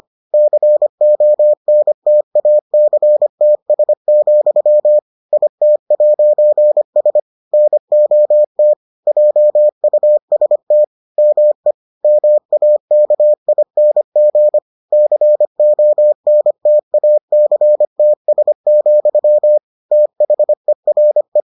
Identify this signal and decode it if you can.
Morse code, heard as a keyed tone, about 25 wpm, 600 Hz. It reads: CONTACTS, IT'S NOT JUST ME MAKING CONTACTS, THERE